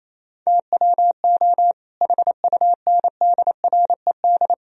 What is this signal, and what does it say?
Morse code, heard as a keyed tone, about 28 wpm, 700 Hz. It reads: TWO HUNDRED
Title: Sentences from Top 1000 Words - Review Part 2 - 28wpm